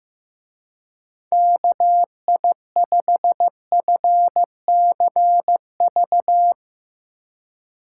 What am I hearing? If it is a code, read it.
Morse code, heard as a keyed tone, about 16 wpm, 700 Hz. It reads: KI5FCV
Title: Speed Racing - USA 2x3 Call Signs 15wpm